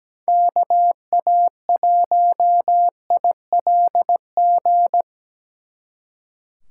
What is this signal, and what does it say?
Morse code, heard as a keyed tone, about 20 wpm, 700 Hz. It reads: KA1ILG
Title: Speed Racing - USA 2x3 Call Signs 17wpm